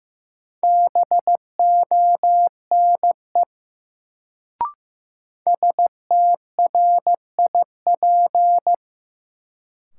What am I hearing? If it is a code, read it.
Morse code, heard as a keyed tone, about 15 wpm, 700 Hz. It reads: BONE STRIP
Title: Sentences from Top 1600 Words - New Word Drill Alt - 15wpm